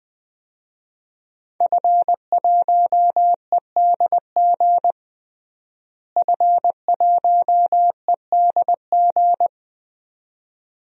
Morse code, heard as a keyed tone, about 20 wpm, 700 Hz. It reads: F1EDG F1EDG
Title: DXCC-100 Call Signs - 5 Characters - Repeated 2x - 20wpm